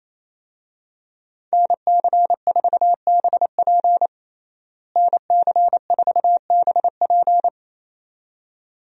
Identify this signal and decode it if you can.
Morse code, heard as a keyed tone, about 28 wpm, 700 Hz. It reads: NC4BP NC4BP
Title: USA Call Signs - 5 Characters - Repeated 2x - 28wpm